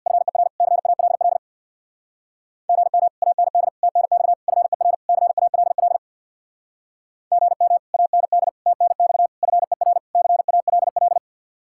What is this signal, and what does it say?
Morse code, heard as a keyed tone, about 63 wpm, 700 Hz. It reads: FER CALL GM AND TNX FER CALL GM AND TNX FER CALL